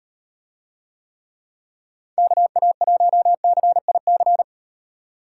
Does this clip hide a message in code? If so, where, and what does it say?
Yes: Morse code from 2.2 to 4.4 s, heard as a keyed tone, about 35 wpm, 700 Hz. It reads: KA1CIC